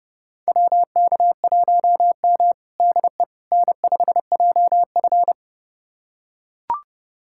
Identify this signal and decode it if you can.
Morse code, heard as a keyed tone, about 30 wpm, 700 Hz. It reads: WK1M DE N5JF
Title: USA Callsign DE Callsign - 4 Characters - 30wpm